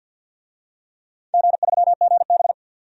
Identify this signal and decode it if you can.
Morse code, heard as a keyed tone, about 47 wpm, 700 Hz. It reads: G3GB
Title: DXCC-100 Call Signs - 4 Characters - 50wpm